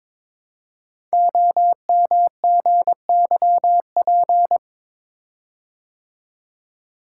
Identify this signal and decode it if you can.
Morse code, heard as a keyed tone, about 22 wpm, 700 Hz. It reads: OMGYP
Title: Code Groups - 5 Letters - 22wpm